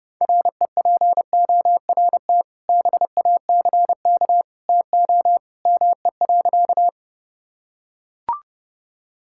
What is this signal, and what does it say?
Morse code, heard as a keyed tone, about 30 wpm, 700 Hz. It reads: REPORT BACK TO ME.
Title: Sentences from Top 1000 Words - Review Part 6 - 30wpm